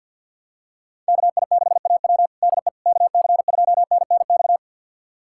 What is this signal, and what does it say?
Morse code, heard as a keyed tone, about 49 wpm, 700 Hz. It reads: KI6AW DE KC2NNX
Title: USA Callsign DE Callsign 50wpm